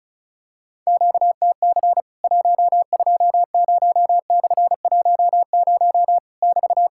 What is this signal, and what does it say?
Morse code, heard as a keyed tone, about 36 wpm, 700 Hz. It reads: QTC 120/10 <BT>
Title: Worked All Europe DX Contest - QTC Exchange - 35wpm